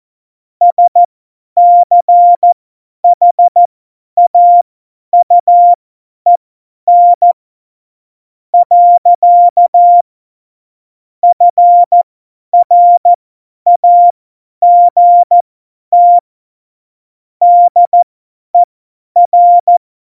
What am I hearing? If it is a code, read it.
Morse code, heard as a keyed tone, about 14 wpm, spaced as for 9 wpm, 700 Hz. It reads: SCHAUEN. FRAGT DER
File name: Morsekurs / Witze / Witz 20 March 2018, 50 F70